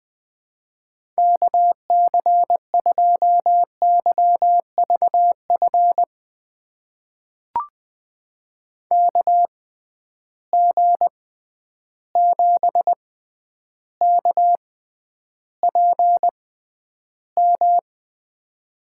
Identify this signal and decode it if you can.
Morse code, heard as a keyed tone, about 20 wpm, spaced as for 10 wpm, 700 Hz. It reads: KC2YVF KG7KPM
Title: USA Call Signs - 6 Characters - 6x Character Spacing - 20wpm